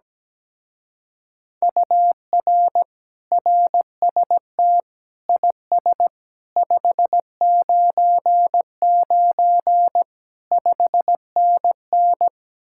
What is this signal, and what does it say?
Morse code, heard as a keyed tone, about 17 wpm, 700 Hz. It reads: UR RST IS 599 5NN
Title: Speed-Racing Common QSO Phrases - 17wpm